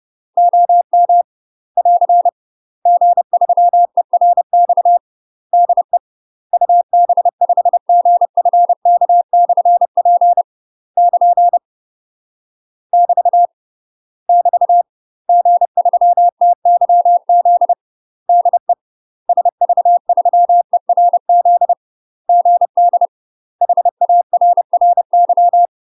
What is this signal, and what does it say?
Morse code, heard as a keyed tone, about 30 wpm, 700 Hz. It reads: OM <AR> G3ERX DE UB5GFK/P <KN> <BT> <BT> G3TYZ DE SV3ERZ GD HARRY
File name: UK MASTER QSOs TRAINING/30WPM